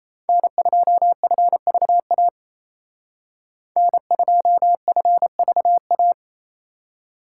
Morse code, heard as a keyed tone, about 36 wpm, 700 Hz. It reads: N2FVA N2FVA